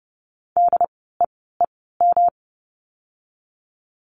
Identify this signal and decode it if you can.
Morse code, heard as a keyed tone, about 30 wpm, spaced as for 17 wpm, 700 Hz. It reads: DEEM
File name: course / 30:10wpm / 108 Lesson 18f - Letter m - Words 30wpm